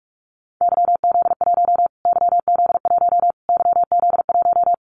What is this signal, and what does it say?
Morse code, heard as a keyed tone, about 44 wpm, 700 Hz. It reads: YZ1 YZ1 YZ1
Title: Callsign Prefixes for Top 25 DXCC - Repeated 3x - 45wpm